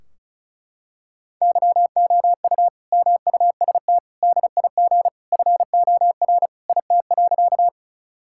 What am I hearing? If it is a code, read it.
Morse code, heard as a keyed tone, about 34 wpm, 700 Hz. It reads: YOU MUST DIG FOR IT.
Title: Sentences from Top 1600 Words 35wpm